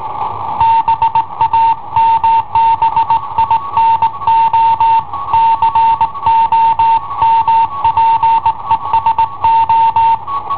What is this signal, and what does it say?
Morse code, heard as a keyed tone, about 18 wpm, 910 Hz. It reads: BAMBINO COMPESO